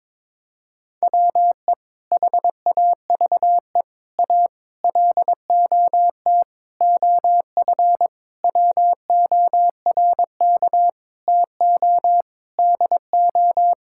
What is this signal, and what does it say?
Morse code, heard as a keyed tone, about 22 wpm, 700 Hz. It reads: WE HAVE A LOT OF WORK TO DO
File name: practice / 22wpm / Sentences with 8-Word Phrases - 22wpm